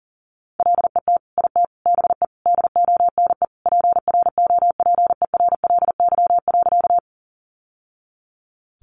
Morse code, heard as a keyed tone, about 40 wpm, 700 Hz. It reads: LET IT BE DONE PROPERLY.
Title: Sentences from Top 1100 Words 40wpm